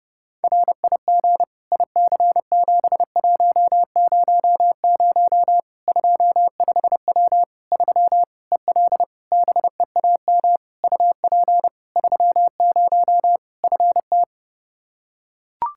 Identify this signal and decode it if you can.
Morse code, heard as a keyed tone, about 32 wpm, 700 Hz. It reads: RIG IC7100 25W 3 EL BEAM UP 30 FT
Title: Practice QSO - Exchange 2 - Info - 30wpm